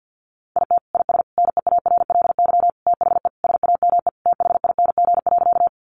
Morse code, heard as a keyed tone, about 50 wpm, 700 Hz. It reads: IT IS NEARLY THE SAME THING.
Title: Sentences from Top 1300 Words 50wpm